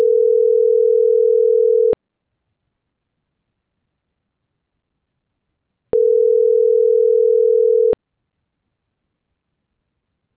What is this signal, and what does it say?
Ringback tone (North America), 440 + 480 Hz.